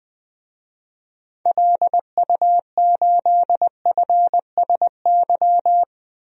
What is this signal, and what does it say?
Morse code, heard as a keyed tone, about 21 wpm, 700 Hz. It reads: LU8FSY
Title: DXCC-100 Call Signs - 6 Characters - Repeated 2x - 20wpm